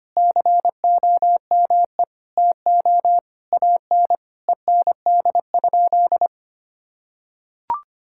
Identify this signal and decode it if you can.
Morse code, heard as a keyed tone, about 25 wpm, 700 Hz. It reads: COME TO AN END?
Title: Sentences from Top 1000 Words - Review Part 5 - 25wpm